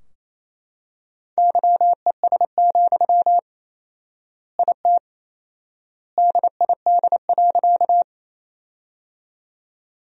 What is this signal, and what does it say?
Morse code, heard as a keyed tone, about 27 wpm, 700 Hz. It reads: YES, IT DID.
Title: 4x Word-Spacing - Sentences from Top 500 Words - 28wpm